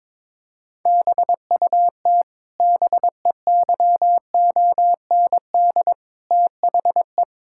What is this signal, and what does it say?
Morse code, heard as a keyed tone, about 22 wpm, 700 Hz. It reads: BUT BEYOND THE
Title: Sentences from Top 1000 Words - Review Part 6 - 22wpm